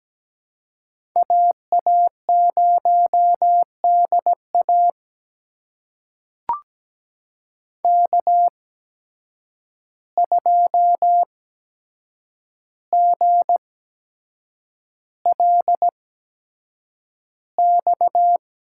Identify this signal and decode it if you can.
Morse code, heard as a keyed tone, about 17 wpm, spaced as for 7 wpm, 700 Hz. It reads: AA0DA K2GLX
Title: USA Call Signs - 5 Characters - 8x Character Spacing - 17wpm